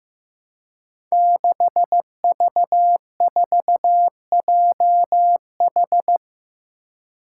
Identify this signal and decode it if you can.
Morse code, heard as a keyed tone, about 15 wpm, 700 Hz. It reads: 6V4JH
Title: Five Challenging Letter-Number Combinations 15wpm